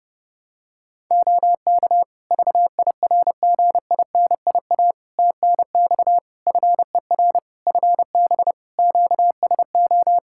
Morse code, heard as a keyed tone, about 28 wpm, 700 Hz. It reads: OK VIRGINIA TNX FER FB QSO
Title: Practice QSO - Exchange 4 - Ending - 30wpm